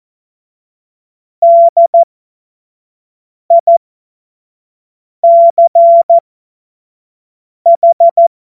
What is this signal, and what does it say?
Morse code, heard as a keyed tone, about 14 wpm, spaced as for 5 wpm, 680 Hz. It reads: DICH
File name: Morsekurs / Witze / Witz 31.Oktober 2023 25 F70